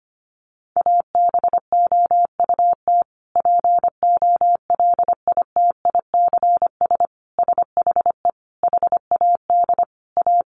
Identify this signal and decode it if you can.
Morse code, heard as a keyed tone, about 25 wpm, 700 Hz. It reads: ABOUT POLITICS SHE HAD A
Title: Sentences from Top 1000 Words 25wpm